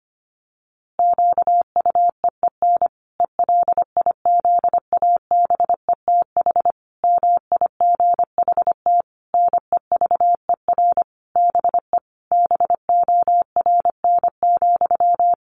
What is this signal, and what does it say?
Morse code, heard as a keyed tone, about 25 wpm, 700 Hz. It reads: QUEEN ELIZABETH MIGHT NEVER BE BORN,
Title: A Sound Of Thunder 25wpm